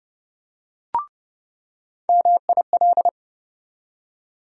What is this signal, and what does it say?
Morse code, heard as a keyed tone, about 30 wpm, 700 Hz. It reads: MIL
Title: USA Call Sign Postfixes - 3 Letters 30wpm